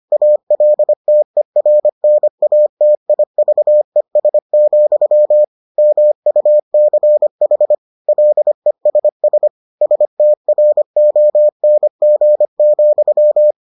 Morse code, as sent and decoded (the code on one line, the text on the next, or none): .- .-.. - . .-. -. .- - .. ...- . ... --..-- / -- ..- -.-. .... / .-.. ... ... / ... - .-. --- -. --. --..--
ALTERNATIVES, MUCH LESS STRONG,